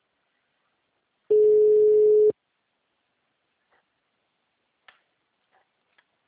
Ringback tone (Europe, 425 Hz).